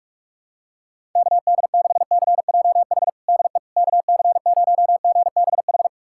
Morse code, heard as a keyed tone, about 42 wpm, 700 Hz. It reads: KD6CJH DE KC0GBH
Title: USA Callsign DE Callsign 45wpm